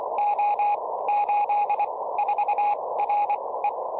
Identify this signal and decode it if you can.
Morse code, heard as a keyed tone, about 25 wpm, spaced as for 18 wpm, 830 Hz, with radio noise. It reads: O84RE
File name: tropo / sp1dfz